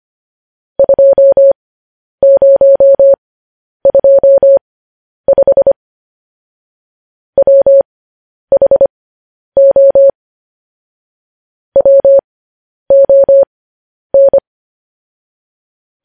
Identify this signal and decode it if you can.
Morse code, heard as a keyed tone, about 25 wpm, spaced as for 10 wpm, 550 Hz. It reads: 2025 WHO WON
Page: news-headlines-in-morse-code-at-10-wpm